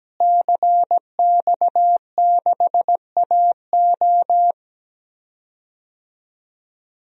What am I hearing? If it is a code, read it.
Morse code, heard as a keyed tone, about 17 wpm, 700 Hz. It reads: CX6AO
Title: DXCC-100 Call Signs - 5 Characters - Repeated 3x - 17wpm